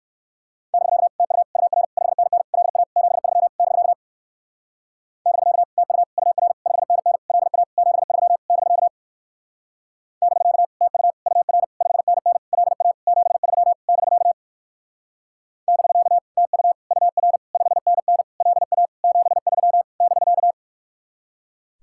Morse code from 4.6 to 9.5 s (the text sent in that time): <BK> TU UR 5NN LA 73 <BK>